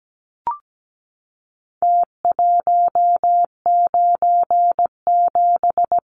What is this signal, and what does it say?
Morse code, heard as a keyed tone, about 17 wpm, 700 Hz. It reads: T197
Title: Numbers - Contest Style - 4-Digit 17wpm